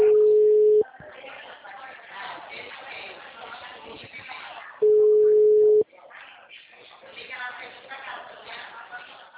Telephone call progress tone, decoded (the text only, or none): ringback tone